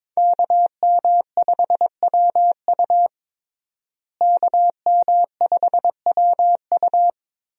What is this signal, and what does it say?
Morse code, heard as a keyed tone, about 22 wpm, 700 Hz. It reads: KM5WU KM5WU
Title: USA Call Signs - 5 Characters - Repeated 2x - 22wpm